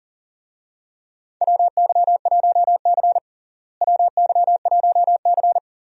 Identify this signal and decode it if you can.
Morse code, heard as a keyed tone, about 40 wpm, 700 Hz. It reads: WY1C WY1C
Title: USA Call Signs - 4 Characters - Repeated 3x - 40wpm